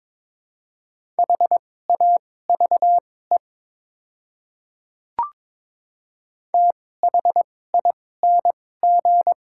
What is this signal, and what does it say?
Morse code, heard as a keyed tone, about 22 wpm, spaced as for 16 wpm, 700 Hz. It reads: HAVE THING
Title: 2x Character Spacing - Top 100 Words - 22wpm